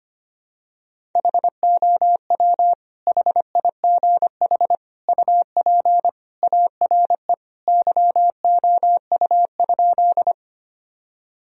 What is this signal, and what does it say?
Morse code, heard as a keyed tone, about 25 wpm, 700 Hz. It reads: HOW HIGH UP ARE YOU?
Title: Sentences from Top 1000 Words - Review Part 1 - 25wpm